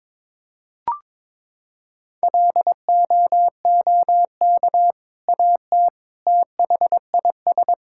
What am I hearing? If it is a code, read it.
Morse code, heard as a keyed tone, about 22 wpm, 700 Hz. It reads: LOOK AT THIS
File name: practice / 22wpm / Sentences with 2-Word Phrases - 22wpm